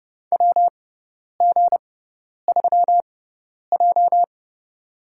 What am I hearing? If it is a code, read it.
Morse code, heard as a keyed tone, about 30 wpm, spaced as for 10 wpm, 700 Hz. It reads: WG3J
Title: USA Call Signs - 6x Character Spacing - 30wpm